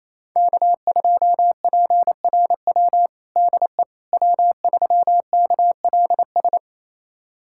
Morse code, heard as a keyed tone, about 27 wpm, 700 Hz. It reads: K2PRW DE W3KLS
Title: USA Callsign DE Callsign - 5 Characters - 28wpm